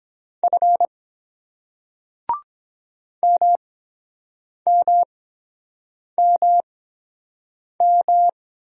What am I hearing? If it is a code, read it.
Morse code, heard as a keyed tone, about 23 wpm, spaced as for 7 wpm, 700 Hz. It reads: F MMMM